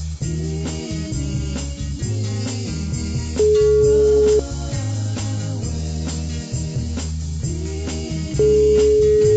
Ringback tone (Europe, 425 Hz).